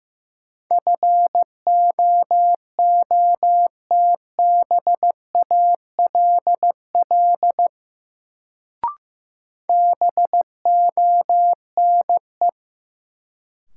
Morse code, heard as a keyed tone, about 16 wpm, 700 Hz. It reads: FOOTBALL BONE
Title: Sentences from Top 1600 Words - New Word Drill - 15wpm